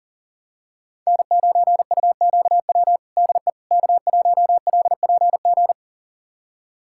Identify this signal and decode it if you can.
Morse code, heard as a keyed tone, about 40 wpm, 700 Hz. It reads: N9UQW DE K1LPG
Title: USA Callsign DE Callsign - 5 Characters - 40wpm